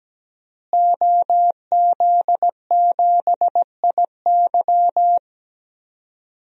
Morse code, heard as a keyed tone, about 17 wpm, 700 Hz. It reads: OZ7IY
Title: DXCC-100 Call Signs - 5 Characters - Repeated 3x - 17wpm